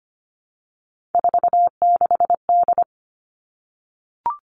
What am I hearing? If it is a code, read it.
Morse code, heard as a keyed tone, about 25 wpm, 700 Hz. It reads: 46D
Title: All Cut Numbers - 3-Digit 25wpm